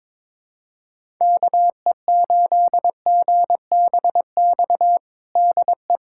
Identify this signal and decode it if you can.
Morse code, heard as a keyed tone, about 22 wpm, 700 Hz. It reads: KE8GBX DE
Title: USA Callsign DE Callsign - 6 Characters - 22wpm